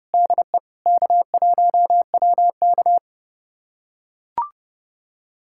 Morse code, heard as a keyed tone, about 32 wpm, 700 Hz. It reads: DE K1WK
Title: USA Callsign DE Callsign - 4 Characters - 30wpm